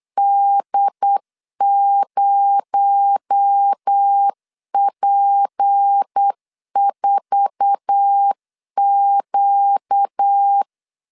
Morse code, heard as a keyed tone, about 8 wpm, 800 Hz. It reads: D0P4Q
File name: LEZIONI / 9 28 8wpm 3512